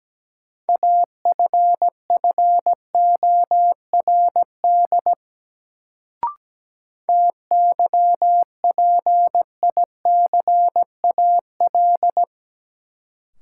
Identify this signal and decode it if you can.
Morse code, heard as a keyed tone, about 17 wpm, 700 Hz. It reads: AFFORD TYPICAL
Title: Sentences from Top 1100 Words - New Word Drill - 17wpm